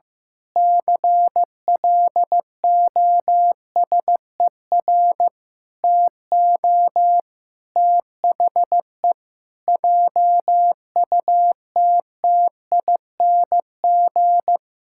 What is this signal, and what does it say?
Morse code, heard as a keyed tone, about 15 wpm, 700 Hz. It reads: CLOSER TO THE JUTTING
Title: Pangram Challenge v2 - 15wpm